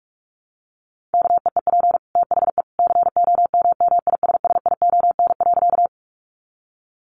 Morse code, heard as a keyed tone, about 44 wpm, 700 Hz. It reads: KEEP THE COMMISSION.
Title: Sentences from Top 1400 Words 45wpm